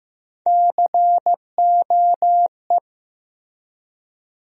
Morse code, heard as a keyed tone, about 15 wpm, 700 Hz. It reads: COE
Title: USA Call Sign Postfixes - 3 Letters 15wpm